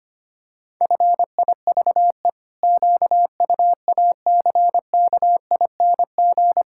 Morse code, heard as a keyed tone, about 25 wpm, 700 Hz. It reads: FIVE QUACKING
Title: Pangram Challenge v2 - 25wpm